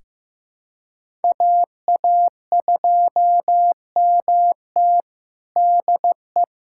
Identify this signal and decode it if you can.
Morse code, heard as a keyed tone, about 15 wpm, 700 Hz. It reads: AA2MT DE
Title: USA Callsign DE Callsign 15wpm